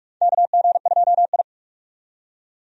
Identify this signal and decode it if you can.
Morse code, heard as a keyed tone, about 46 wpm, 700 Hz. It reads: KG2I
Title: USA Call Signs - 4 Characters - 45wpm